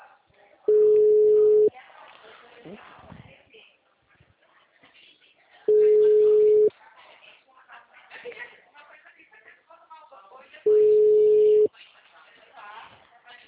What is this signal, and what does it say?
Ringback tone (Europe, 425 Hz), with speech.
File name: out- -2010- - - .575